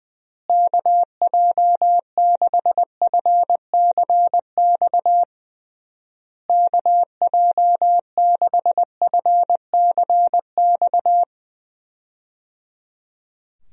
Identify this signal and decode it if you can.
Morse code, heard as a keyed tone, about 20 wpm, 700 Hz. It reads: KJ6FCX KJ6FCX
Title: USA Call Signs - 6 Characters - Repeated 3x - 20wpm